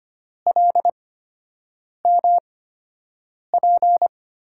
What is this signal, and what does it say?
Morse code, heard as a keyed tone, about 25 wpm, spaced as for 7 wpm, 700 Hz. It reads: LMP
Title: USA Call Signs - 8x Character Spacing - 25wpm